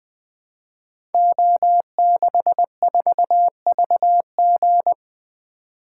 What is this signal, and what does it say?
Morse code, heard as a keyed tone, about 20 wpm, 700 Hz. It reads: O64VG
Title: Code Groups - 5 Letter:Numbers - 20wpm